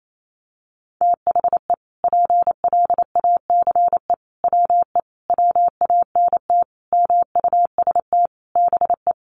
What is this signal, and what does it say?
Morse code, heard as a keyed tone, about 28 wpm, 700 Hz. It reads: THE PLACE WE WANT MUST BE
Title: Sentences from Top 1400 Words 28wpm